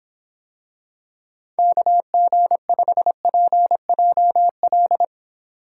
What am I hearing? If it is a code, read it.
Morse code, heard as a keyed tone, about 24 wpm, 700 Hz. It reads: KG5PJL